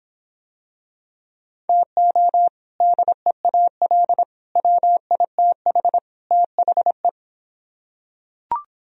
Morse code, heard as a keyed tone, about 25 wpm, 700 Hz. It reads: TO DEAL WITH THE